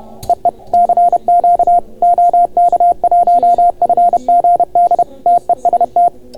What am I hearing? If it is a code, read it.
Morse code, heard as a keyed tone, about 31 wpm, 690 Hz. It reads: EE CQ OK1FGD TEST